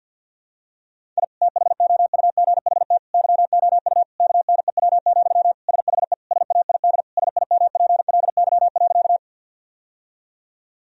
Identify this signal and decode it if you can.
Morse code, heard as a keyed tone, about 51 wpm, 700 Hz. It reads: I THOUGHT YOU KNEW, SHE SAID SIMPLY.
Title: Sentences from Top 1000 Words - Review Part 4 - 50wpm